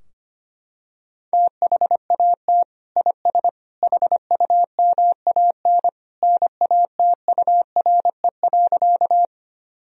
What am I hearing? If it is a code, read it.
Morse code, heard as a keyed tone, about 25 wpm, 700 Hz. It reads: THAT IS HUMAN NATURE.
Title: Sentences from Top 1000 Words - Review Part 3 - 25wpm